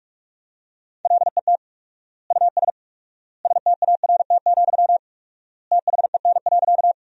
Morse code, heard as a keyed tone, about 46 wpm, 700 Hz. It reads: LET US START, THEN.
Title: 4x Word-Spacing - Sentences from Top 300 Words - 45wpm